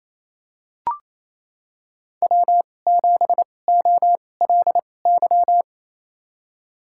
Morse code, heard as a keyed tone, about 28 wpm, spaced as for 20 wpm, 700 Hz. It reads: W7OLY
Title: USA Call Signs - 5 Characters - 2x Character Spacing - 28wpm